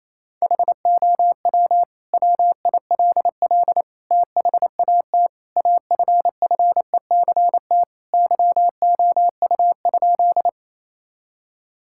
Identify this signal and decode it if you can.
Morse code, heard as a keyed tone, about 28 wpm, 700 Hz. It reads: HOW WILL THAT AFFECT YOU?
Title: Sentences from Top 1000 Words - Review Part 6 - 28wpm